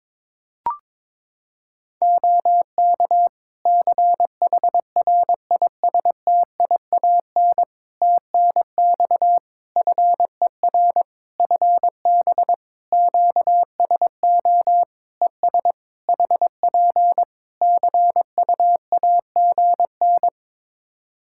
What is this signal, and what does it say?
Morse code, heard as a keyed tone, about 22 wpm, 700 Hz. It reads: OK CHRISTIAN TNX FER FB QSO ES HP CUAGN
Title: Practice QSO - Exchange 4 - Ending - 22wpm